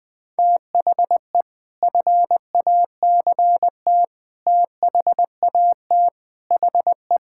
Morse code, heard as a keyed tone, about 20 wpm, 700 Hz. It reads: THE FACT THAT HE
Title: Sentences with 4-Word Phrases - Spoken Beginning - 20wpm